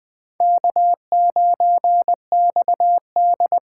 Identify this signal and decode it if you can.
Morse code, heard as a keyed tone, about 20 wpm, 700 Hz. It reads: K9XD
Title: K1USN SST QSO Practice - 20wpm